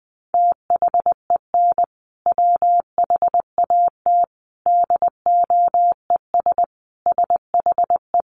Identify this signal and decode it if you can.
Morse code, heard as a keyed tone, about 20 wpm, 700 Hz. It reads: THEN WHAT DOES SHE
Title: Sentences from Top 1500 Words 20wpm